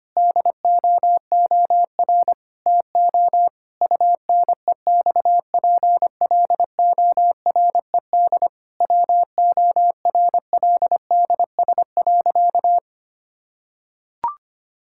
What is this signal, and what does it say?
Morse code, heard as a keyed tone, about 25 wpm, 700 Hz. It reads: DOOR TO UNEXPLORED WORLDS.
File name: practice / 25wpm / Difficult Sentences with Hints - 25wpm